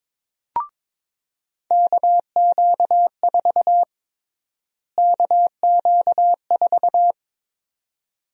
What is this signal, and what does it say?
Morse code, heard as a keyed tone, about 22 wpm, 700 Hz. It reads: KQ4 KQ4
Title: USA Call Sign Prefixes - Repeated 2x - 22wpm